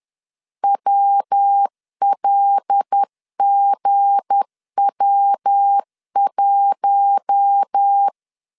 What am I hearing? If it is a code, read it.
Morse code, heard as a keyed tone, about 10 wpm, 800 Hz. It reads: WLGW1
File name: LEZIONI / 7 GF9Z / 7 GF9Z 10wpm 2611